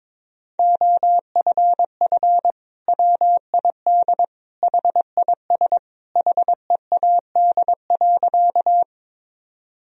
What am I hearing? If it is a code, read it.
Morse code, heard as a keyed tone, about 21 wpm, 700 Hz. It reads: OFF WID HIS HEAD.
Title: Sentences from Top 300 Words and Abbreviations 22wpm